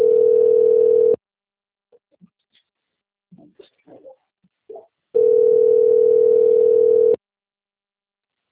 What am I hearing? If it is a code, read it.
Ringback tone (North America), 440 + 480 Hz.
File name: rg-1600-unknown- - - .32840